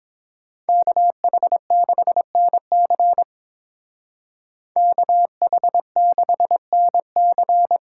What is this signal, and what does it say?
Morse code, heard as a keyed tone, about 28 wpm, 700 Hz. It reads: KH6NC KH6NC